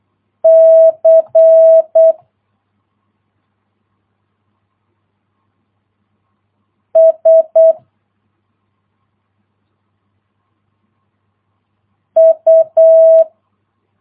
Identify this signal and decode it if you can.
Morse code, heard as a keyed tone, about 8 wpm, 650 Hz. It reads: C S U